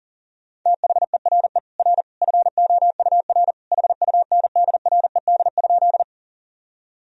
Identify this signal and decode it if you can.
Morse code, heard as a keyed tone, about 40 wpm, 700 Hz. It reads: THERE R FOUR HUNDRED?
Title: Sentences from Top 400 Words and Abbreviations 40wpm